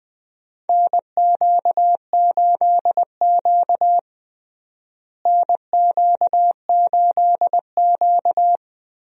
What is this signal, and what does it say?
Morse code, heard as a keyed tone, about 21 wpm, 700 Hz. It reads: NQ8Q NQ8Q
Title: USA Call Signs - 4 Characters - Repeated 3x - 20wpm